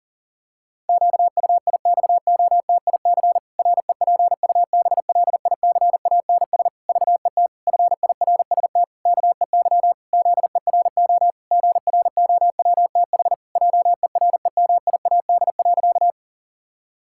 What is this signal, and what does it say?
Morse code, heard as a keyed tone, about 40 wpm, 700 Hz. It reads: QUIXOTIC REPUBLICANS VET FIRST KEY ZERO GROWTH JEREMIAD.
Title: Pangram Challenge v2 - 40wpm